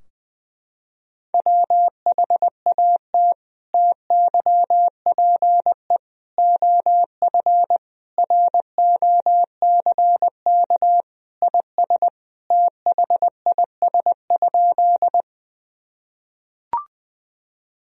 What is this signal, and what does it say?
Morse code, heard as a keyed tone, about 21 wpm, 700 Hz. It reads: WHAT TYPE OF ROCK IS THIS?
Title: Sentences from Top 1000 Words - Review Part 3 - 20wpm